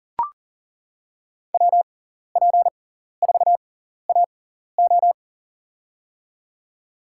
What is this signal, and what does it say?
Morse code, heard as a keyed tone, about 39 wpm, spaced as for 14 wpm, 700 Hz. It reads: WP4AO
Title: USA Call Signs - 5 Characters - 6x Character Spacing - 40wpm